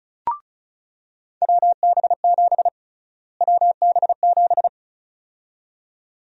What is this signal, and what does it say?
Morse code, heard as a keyed tone, about 36 wpm, 700 Hz. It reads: WB7 WB7
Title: USA Call Sign Prefixes - Repeated 2x - 35wpm